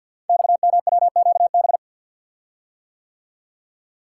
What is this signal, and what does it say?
Morse code, heard as a keyed tone, about 51 wpm, 700 Hz. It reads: XMWQB
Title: Code Groups - 5 Letter:Numbers - 50wpm